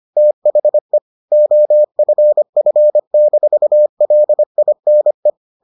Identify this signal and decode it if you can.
Morse code, heard as a keyed tone, about 25 wpm, 600 Hz. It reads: THE OFF-LINE